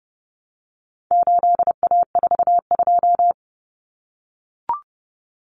Morse code, heard as a keyed tone, about 30 wpm, 700 Hz. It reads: 8A42
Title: Common Cut Numbers - 4-Digit 30wpm